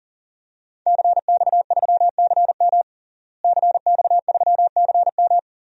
Morse code, heard as a keyed tone, about 37 wpm, 700 Hz. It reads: CX3CM CX3CM
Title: DXCC-100 Call Signs - 5 Characters - Repeated 3x - 40wpm